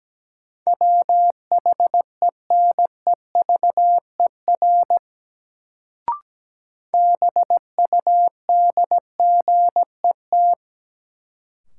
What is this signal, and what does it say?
Morse code, heard as a keyed tone, about 17 wpm, 700 Hz. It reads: WHENEVER BUDGET
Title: Sentences from Top 1200 Words - New Word Drill - 17wpm